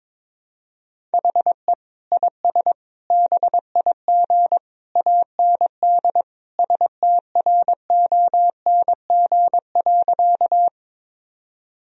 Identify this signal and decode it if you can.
Morse code, heard as a keyed tone, about 22 wpm, 700 Hz. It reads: HE IS BIG AND STRONG.
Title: Sentences from Top 1000 Words - Review Part 3 - 22wpm